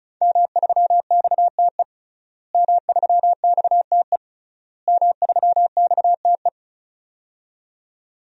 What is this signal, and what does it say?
Morse code, heard as a keyed tone, about 36 wpm, 700 Hz. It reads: M3XTE M3XTE M3XTE
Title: DXCC-100 Call Signs - 5 Characters - Repeated 3x - 35wpm